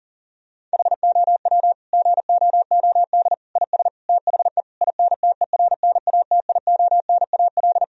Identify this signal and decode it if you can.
Morse code, heard as a keyed tone, about 40 wpm, 700 Hz. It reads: HOW GOOD IS THE INTERNATIONAL
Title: Sentences from Top 1000 Words - Review Part 4 - 40wpm